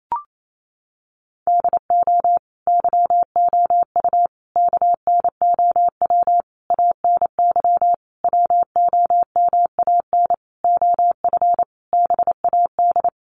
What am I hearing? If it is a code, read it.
Morse code, heard as a keyed tone, about 28 wpm, 700 Hz. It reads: DO YOU KNOW ANY WOMAN OF BAD
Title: Sentences from Top 1200 Words 28wpm